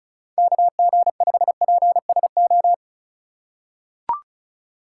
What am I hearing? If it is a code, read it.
Morse code, heard as a keyed tone, about 34 wpm, 700 Hz. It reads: KG5PSO
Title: USA Call Signs - 6 Characters - 2x Character Spacing - 35wpm